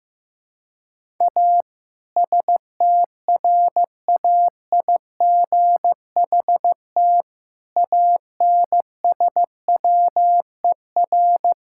Morse code, heard as a keyed tone, about 15 wpm, 700 Hz. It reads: A STRAIGHT ANSWER
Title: Sentences from Top 1000 Words - Review Part 6 - 15wpm